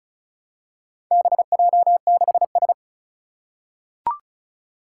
Morse code, heard as a keyed tone, about 36 wpm, 700 Hz. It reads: BJ6S
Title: Four Challenging Letter-Number Combinations 35wpm